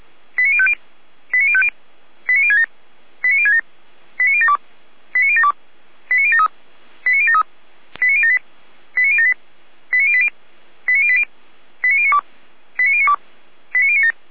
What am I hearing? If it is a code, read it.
Five-tone selective call (ZVEI), heard as quick runs of five tones, tones of about 70 ms. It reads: 89050 89050 89067 89067 89072 89072 89073 89073 89078 89078 89080 89080 89092 89092 89097
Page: ZVEI